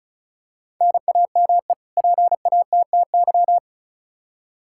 Morse code, heard as a keyed tone, about 34 wpm, 700 Hz. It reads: NAME PATTY
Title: K1USN SST QSO Practice 35wpm